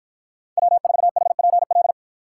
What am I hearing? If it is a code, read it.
Morse code, heard as a keyed tone, about 53 wpm, 700 Hz. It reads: W4HPL